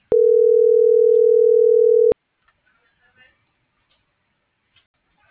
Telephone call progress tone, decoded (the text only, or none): ringback tone